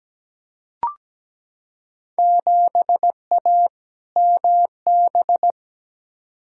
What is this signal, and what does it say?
Morse code, heard as a keyed tone, about 17 wpm, 700 Hz. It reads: 7A MB
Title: Field Day Exchanges Encore - 17wpm